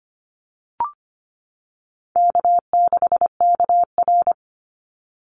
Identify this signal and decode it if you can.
Morse code, heard as a keyed tone, about 25 wpm, 700 Hz. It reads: K6KR
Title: Easy Call Signs - 25wpm